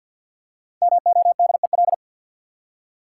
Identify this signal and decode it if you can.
Morse code, heard as a keyed tone, about 53 wpm, 700 Hz. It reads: MODEL
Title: Five Letter Words - 50wpm